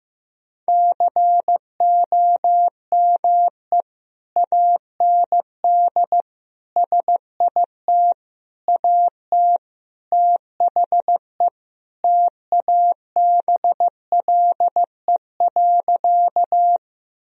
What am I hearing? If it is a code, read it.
Morse code, heard as a keyed tone, about 16 wpm, 700 Hz. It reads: COME AND SIT AT THE TABLE.
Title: Sentences from Top 1000 Words - Review Part 2 - 15wpm